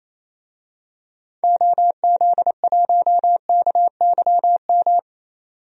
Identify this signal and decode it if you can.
Morse code, heard as a keyed tone, about 28 wpm, 700 Hz. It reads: OZ1KYM
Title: DXCC-100 Call Signs - 6 Characters - 28wpm